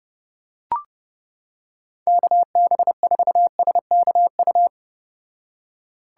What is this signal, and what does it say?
Morse code, heard as a keyed tone, about 31 wpm, 700 Hz. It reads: KB4SKU